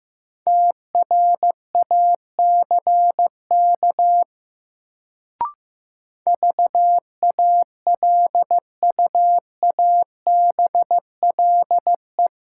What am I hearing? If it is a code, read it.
Morse code, heard as a keyed tone, about 15 wpm, 700 Hz. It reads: TRACK VALUABLE
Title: Sentences from Top 1100 Words - New Word Drill - 15wpm